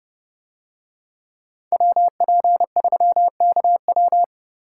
Morse code, heard as a keyed tone, about 29 wpm, 700 Hz. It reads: WP3KW